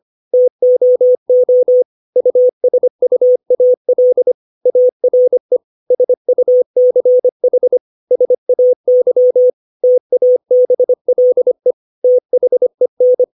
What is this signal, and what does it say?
Morse code, heard as a keyed tone, about 25 wpm, 500 Hz. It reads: TOO USUAL ARE SUCH SAY TABLE THEN
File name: 632 words / 632 Words 25WPM